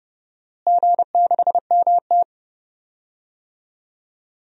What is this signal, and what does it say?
Morse code, heard as a keyed tone, about 29 wpm, 700 Hz. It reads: G6MT
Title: DXCC-100 Call Signs - 4 Characters - Repeated 3x - 30wpm